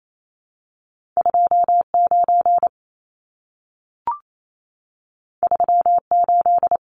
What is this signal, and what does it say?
Morse code, heard as a keyed tone, about 28 wpm, 700 Hz. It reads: 29 38
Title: Numbers - 2-Digit 28wpm